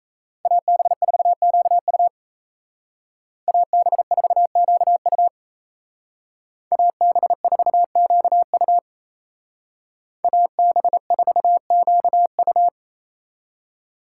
Morse code, heard as a keyed tone, about 36 wpm, 700 Hz. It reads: AB4QU AB4QU AB4QU AB4QU